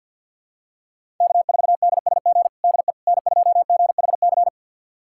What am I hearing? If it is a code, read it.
Morse code, heard as a keyed tone, about 53 wpm, 700 Hz. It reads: K4DSG DE N2GHC
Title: USA Callsign DE Callsign - 5 Characters - 50wpm